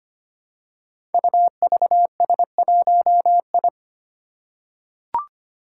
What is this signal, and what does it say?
Morse code, heard as a keyed tone, about 25 wpm, 700 Hz. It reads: UVS1I
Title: Five Challenging Letter-Number Combinations 25wpm